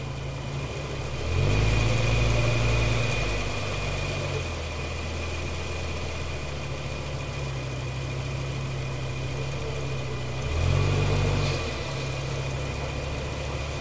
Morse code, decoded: none